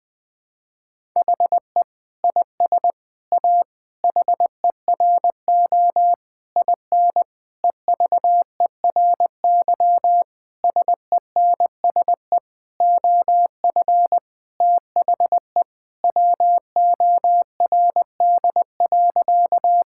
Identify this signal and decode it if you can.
Morse code, heard as a keyed tone, about 20 wpm, 700 Hz. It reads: HE IS A HERO IN EVERY SENSE OF THE WORD.
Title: Sentences with 6-Word Phrases - 20wpm